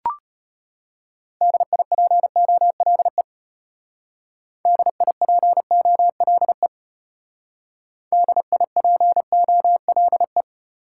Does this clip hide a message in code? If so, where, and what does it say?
Yes: Morse code from 1.4 to 10.4 s, heard as a keyed tone, about 32 wpm, 700 Hz. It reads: DIPOLE DIPOLE DIPOLE